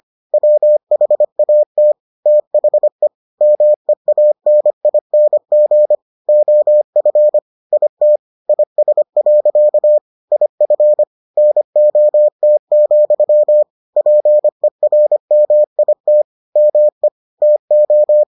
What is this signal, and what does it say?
Morse code, heard as a keyed tone, about 25 wpm, 600 Hz. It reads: WHAT THE MEANING OF IT IS. IF NOT, PERMIT ME TO